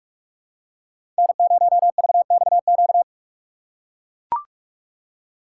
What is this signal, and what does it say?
Morse code, heard as a keyed tone, about 42 wpm, 700 Hz. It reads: N0VXQ
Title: USA Call Signs - 5 Characters - 4x Character Spacing - 45wpm